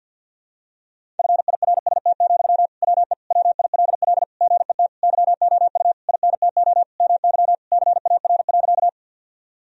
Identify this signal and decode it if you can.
Morse code, heard as a keyed tone, about 52 wpm, 700 Hz. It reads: FIRST, WE WILL GET YOU INTO MY CAR.
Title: Sentences from Top 1000 Words - Review Part 1 - 50wpm